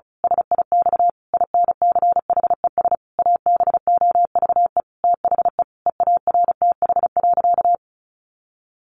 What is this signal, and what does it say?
Morse code, heard as a keyed tone, about 34 wpm, 700 Hz. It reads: SIX INCHES ABOVE THE EARTH.
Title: A Sound Of Thunder 35wpm